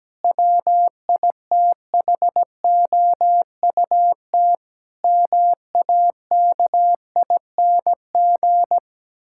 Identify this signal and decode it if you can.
Morse code, heard as a keyed tone, about 17 wpm, 700 Hz. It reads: WITHOUT MAKING
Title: Sentences from Top 1000 Words - Review Part 6 - 17wpm